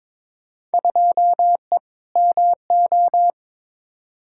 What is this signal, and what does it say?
Morse code, heard as a keyed tone, about 22 wpm, 700 Hz. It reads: 2E MO
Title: Field Day Exchanges - 22wpm